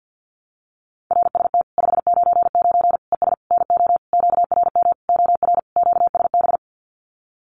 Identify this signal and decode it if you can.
Morse code, heard as a keyed tone, about 50 wpm, 700 Hz. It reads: RST 599 ES NO QRM OR QSB
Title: Common QSO Phrases 50wpm